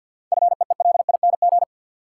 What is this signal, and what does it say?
Morse code, heard as a keyed tone, about 53 wpm, 700 Hz. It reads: FEELING
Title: Sentences from Top 1200 Words - New Word Drill Alt - 50wpm